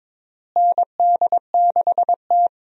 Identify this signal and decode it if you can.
Morse code, heard as a keyed tone, about 22 wpm, 700 Hz. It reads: ND6T
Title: USA Call Signs - 4 Characters - Repeated 2x - 22wpm